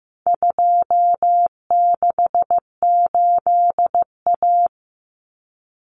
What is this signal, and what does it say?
Morse code, heard as a keyed tone, about 15 wpm, 700 Hz. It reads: 268A
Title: Common Cut Numbers - 4-Digit 15wpm